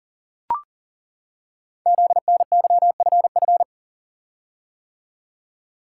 Morse code, heard as a keyed tone, about 42 wpm, 700 Hz. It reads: ZNYFF
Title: Code Groups - 5 Letters - 40wpm